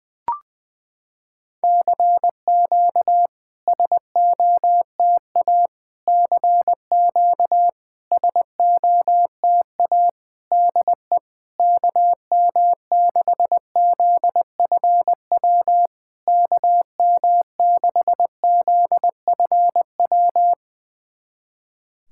Morse code, heard as a keyed tone, about 20 wpm, 700 Hz. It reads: CQ SOTA CQ SOTA DE KM6ZFW KM6ZFW
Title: SOTA QSO Practice 20wpm